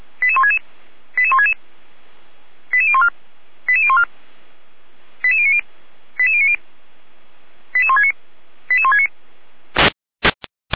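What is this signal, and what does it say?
Five-tone selective call (ZVEI), heard as quick runs of five tones, tones of about 70 ms. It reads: 88166 88166 88014 88014 88090 88090 88169 88169
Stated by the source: ZVEI five-tone call received by SDR.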